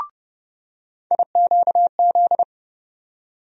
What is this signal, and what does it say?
Morse code, heard as a keyed tone, about 30 wpm, 700 Hz. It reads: IQZ
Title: USA Call Sign Postfixes - 3 Letters 30wpm